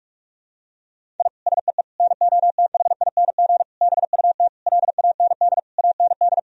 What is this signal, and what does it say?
Morse code, heard as a keyed tone, about 46 wpm, 700 Hz. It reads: I SEE NOTHING BUT LAND AND
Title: Sentences from Top 1000 Words - Review Part 1 - 45wpm